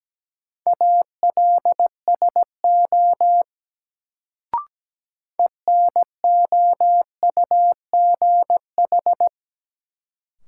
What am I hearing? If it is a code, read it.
Morse code, heard as a keyed tone, about 17 wpm, 700 Hz. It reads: ALSO ENOUGH
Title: Sentences from Top 400 Words - New Word Drill - 17wpm